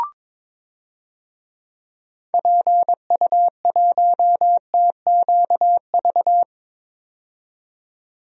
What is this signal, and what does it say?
Morse code, heard as a keyed tone, about 22 wpm, 700 Hz. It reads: PU1TQV
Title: DXCC-100 Call Signs - 6 Characters - 22wpm